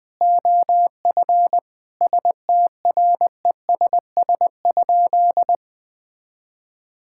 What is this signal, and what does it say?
Morse code, heard as a keyed tone, about 20 wpm, 700 Hz. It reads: OF STRESS?
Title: Sentences from Top 1000 Words - Review Part 6 - 20wpm